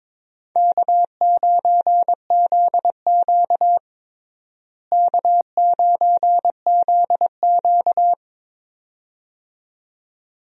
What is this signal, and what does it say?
Morse code, heard as a keyed tone, about 22 wpm, 700 Hz. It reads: K9ZQ K9ZQ
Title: USA Call Signs - 4 Characters - Repeated 2x - 22wpm